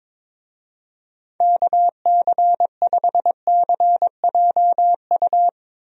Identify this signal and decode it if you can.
Morse code, heard as a keyed tone, about 22 wpm, 700 Hz. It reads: KC5CJU
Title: USA Call Signs - 6 Characters - Repeated 3x - 22wpm